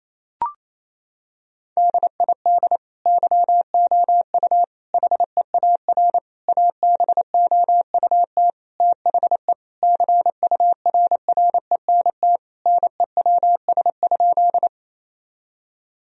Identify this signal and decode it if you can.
Morse code, heard as a keyed tone, about 28 wpm, 700 Hz. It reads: DID YOU HEAR ABOUT THE CURRENT NEWS?
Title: Sentences from Top 1000 Words - Review Part 4 - 28wpm